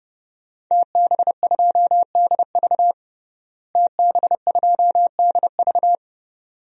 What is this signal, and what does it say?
Morse code, heard as a keyed tone, about 32 wpm, 700 Hz. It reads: TB2DV TB2DV
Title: DXCC-100 Call Signs - 5 Characters - Repeated 2x - 30wpm